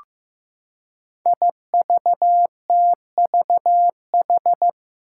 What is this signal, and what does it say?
Morse code, heard as a keyed tone, about 15 wpm, 700 Hz. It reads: IVTVH
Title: Five Challenging Letter-Number Combinations 15wpm